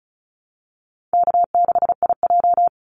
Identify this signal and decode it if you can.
Morse code, heard as a keyed tone, about 34 wpm, 700 Hz. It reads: K6IJ
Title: Easy Call Signs Repeated 3x - 35wpm